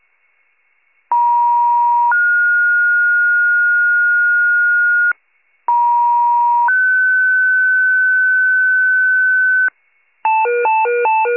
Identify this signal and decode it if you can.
Two-tone page, heard as a one-second tone, then a three-second tone, about 950 then 1430 Hz, about 950 then 1500 Hz, with noise. It is heard from 1.1 to 9.7 s.